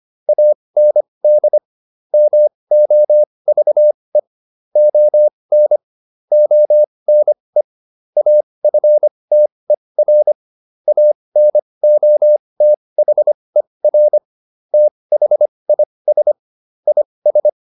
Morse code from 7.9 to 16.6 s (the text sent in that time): AFTER ANOTHER THIS